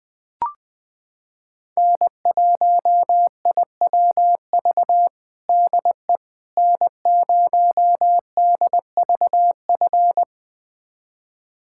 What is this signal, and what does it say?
Morse code, heard as a keyed tone, about 21 wpm, 700 Hz. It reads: N1IWV DE N0DVF
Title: USA Callsign DE Callsign - 5 Characters - 20wpm